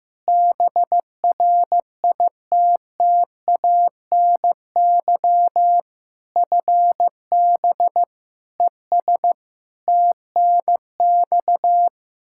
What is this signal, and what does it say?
Morse code, heard as a keyed tone, about 15 wpm, 700 Hz. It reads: BRITTANY FB ES TNX
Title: Practice QSO - Exchange 2 - Info - 15wpm